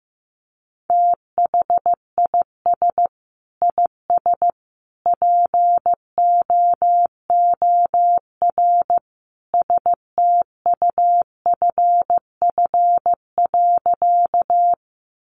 Morse code, heard as a keyed tone, about 15 wpm, 700 Hz. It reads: THIS IS POOR STUFF.
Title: Sentences from Top 1400 Words 15wpm